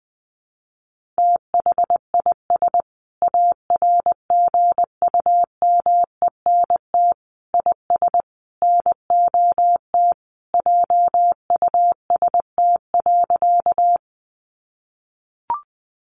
Morse code, heard as a keyed tone, about 20 wpm, 700 Hz. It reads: THIS ARGUMENT IS NOT JUST.
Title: Sentences from Top 1400 Words 20wpm